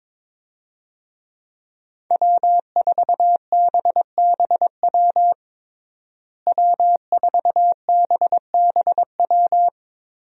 Morse code, heard as a keyed tone, about 21 wpm, 700 Hz. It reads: W4BBW W4BBW
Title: USA Call Signs - 5 Characters - Repeated 3x - 22wpm